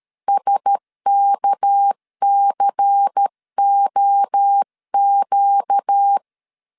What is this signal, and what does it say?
Morse code, heard as a keyed tone, about 13 wpm, 800 Hz. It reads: SKCOQ
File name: LEZIONI / 7 GF9Z / 7 GF 12wpm 2122